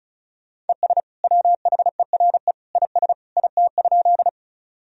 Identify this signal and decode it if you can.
Morse code, heard as a keyed tone, about 36 wpm, 700 Hz. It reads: ES WHERE IS IT?
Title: Sentences from Top 200 Words and Abbreviations 35wpm